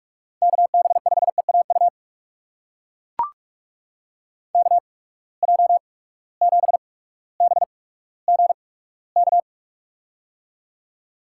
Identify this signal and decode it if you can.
Morse code, heard as a keyed tone, about 44 wpm, spaced as for 19 wpm, 700 Hz. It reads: KB5EAU KJ7BGK
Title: USA Call Signs - 6 Characters - 8x Character Spacing - 45wpm